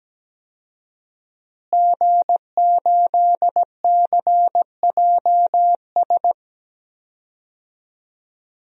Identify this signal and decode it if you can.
Morse code, heard as a keyed tone, about 17 wpm, 700 Hz. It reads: G8CJS
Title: DXCC-100 Call Signs - 5 Characters - Repeated 3x - 17wpm